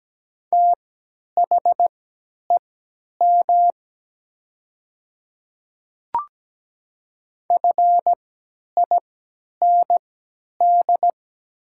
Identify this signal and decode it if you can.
Morse code, heard as a keyed tone, about 17 wpm, spaced as for 10 wpm, 700 Hz. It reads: THEM FIND
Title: 3x Character Spacing - Top 200 Words - 17wpm